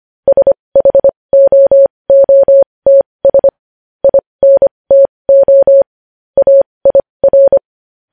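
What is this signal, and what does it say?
Morse code, heard as a keyed tone, about 25 wpm, spaced as for 20 wpm, 550 Hz. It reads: SHOOTS INTO AIR
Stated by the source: Morse code news headlines practice at 20 wpm.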